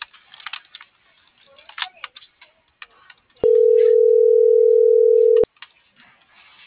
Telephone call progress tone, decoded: ringback tone

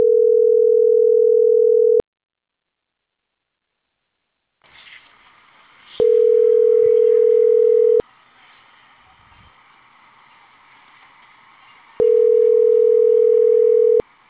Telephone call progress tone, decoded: ringback tone

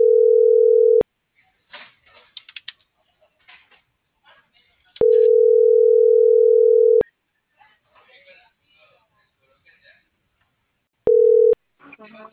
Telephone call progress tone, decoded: ringback tone

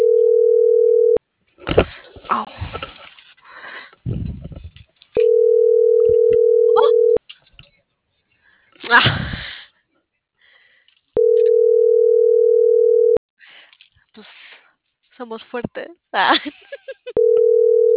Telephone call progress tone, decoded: ringback tone

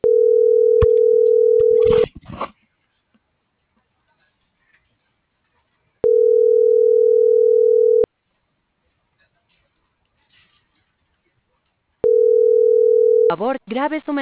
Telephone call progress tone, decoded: ringback tone

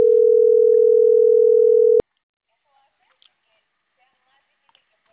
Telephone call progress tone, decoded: ringback tone

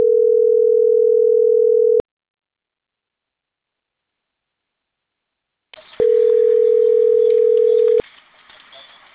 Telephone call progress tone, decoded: ringback tone

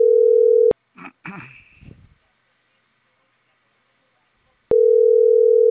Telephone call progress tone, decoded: ringback tone

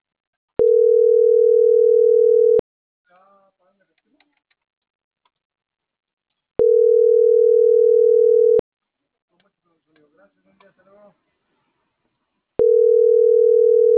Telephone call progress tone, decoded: ringback tone

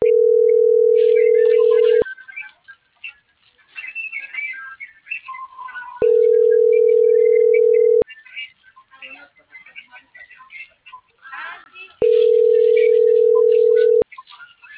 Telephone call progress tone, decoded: ringback tone

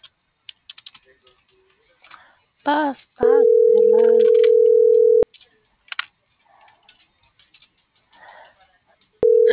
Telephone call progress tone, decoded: ringback tone

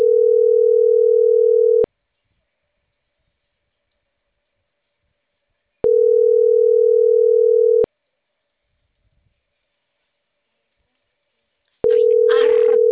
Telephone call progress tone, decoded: ringback tone

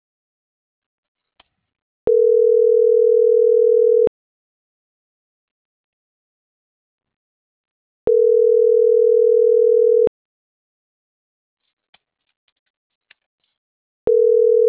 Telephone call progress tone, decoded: ringback tone

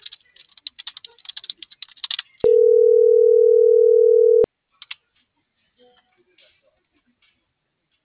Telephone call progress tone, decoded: ringback tone

